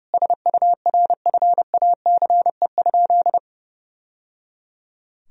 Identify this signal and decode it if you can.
Morse code, heard as a keyed tone, about 30 wpm, 700 Hz. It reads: SURFACE?
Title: Sentences from Top 1000 Words - Review Part 3 - 30wpm